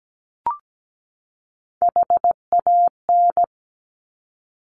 Morse code, heard as a keyed tone, about 17 wpm, 700 Hz. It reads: HAN
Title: Top Three Letter Combinations in Words 17wpm